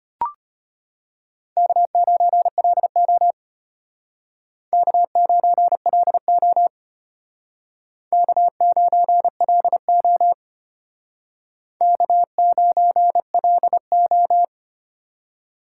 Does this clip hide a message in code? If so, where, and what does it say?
Yes: Morse code from 1.6 to 14.4 s, heard as a keyed tone, about 32 wpm, 700 Hz. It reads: K9LO K9LO K9LO K9LO